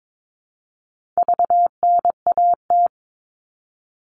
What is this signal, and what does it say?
Morse code, heard as a keyed tone, about 22 wpm, 700 Hz. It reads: VNAT
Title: All Cut Numbers - 4-Digit 22wpm